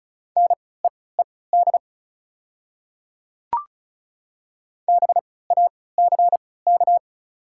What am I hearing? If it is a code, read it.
Morse code, heard as a keyed tone, about 34 wpm, spaced as for 20 wpm, 690 Hz. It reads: NEED BACK
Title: 3x Character Spacing - Top 300 Words - 35wpm